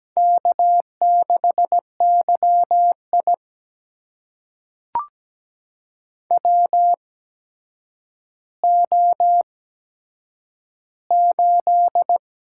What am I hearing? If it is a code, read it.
Morse code, heard as a keyed tone, about 17 wpm, spaced as for 8 wpm, 700 Hz. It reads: K6YI WO8
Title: USA Call Signs - 4 Characters - 8x Character Spacing - 17wpm